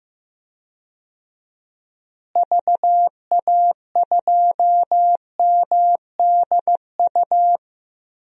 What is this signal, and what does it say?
Morse code, heard as a keyed tone, about 15 wpm, 700 Hz. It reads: VA2MDU
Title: DXCC-100 Call Signs - 6 Characters - Repeated 3x - 15wpm